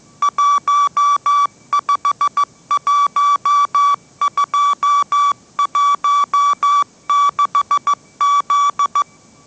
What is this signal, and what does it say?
Morse code, heard as a keyed tone, about 15 wpm, 1150 Hz. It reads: 151216Z
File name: mrx / mrx065 16wpm